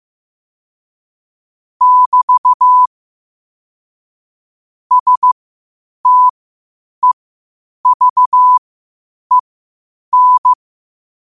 Morse code, heard as a keyed tone, about 15 wpm, spaced as for 8 wpm, 1000 Hz. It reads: <BT> STEVEN
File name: qotdfiles / 8wpm